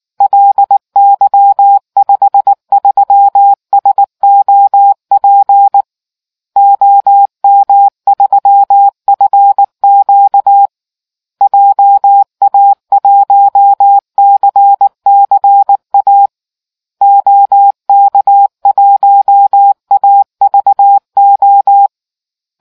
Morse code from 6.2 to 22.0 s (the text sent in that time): OM3FQ JA1CCA OK1AVO